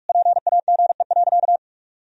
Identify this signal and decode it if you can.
Morse code, heard as a keyed tone, about 46 wpm, 700 Hz. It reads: PAGE.
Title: Sentences from Top 1000 Words - Review Part 1 - 45wpm